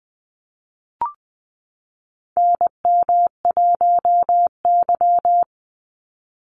Morse code, heard as a keyed tone, about 20 wpm, 700 Hz. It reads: NM1Y
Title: Call Signs - 20wpm